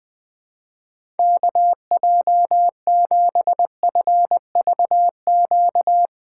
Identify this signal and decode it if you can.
Morse code, heard as a keyed tone, about 20 wpm, 700 Hz. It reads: KJ7FVQ
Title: USA Call Signs - 8x Character Spacing - 20wpm